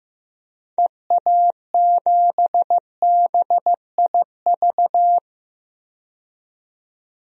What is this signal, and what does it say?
Morse code, heard as a keyed tone, about 15 wpm, 690 Hz. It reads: EA7BIV
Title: DXCC-100 Call Signs - 6 Characters - 15wpm